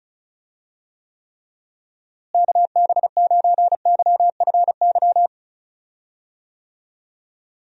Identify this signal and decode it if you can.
Morse code, heard as a keyed tone, about 34 wpm, 700 Hz. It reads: KB9YFY
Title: USA Call Signs - 6 Characters - Repeated 3x - 35wpm